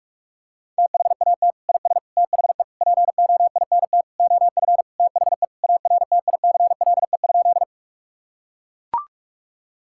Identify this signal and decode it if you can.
Morse code, heard as a keyed tone, about 41 wpm, 700 Hz. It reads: THAT IS THE POINT OF THE ARTICLE?
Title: Sentences from Top 1000 Words - Review Part 5 - 45wpm